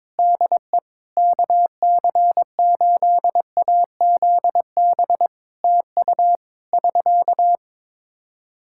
Morse code, heard as a keyed tone, about 22 wpm, 700 Hz. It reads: DE KC8AZB TU <SK>
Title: Practice QSO - Exchange 4 - Ending - 22wpm